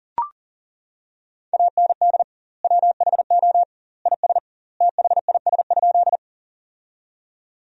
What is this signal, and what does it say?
Morse code, heard as a keyed tone, about 40 wpm, 700 Hz. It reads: AND WHO IS THIS?
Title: 2x Word-Spacing - Sentences from Top 100 Words - 40wpm